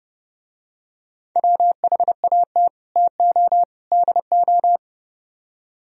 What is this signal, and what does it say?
Morse code, heard as a keyed tone, about 30 wpm, 700 Hz. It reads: WHAT TO DO